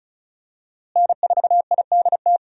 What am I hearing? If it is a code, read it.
Morse code, heard as a keyed tone, about 36 wpm, 700 Hz. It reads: N4IDT
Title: USA Call Signs - 5 Characters - Repeated 3x - 35wpm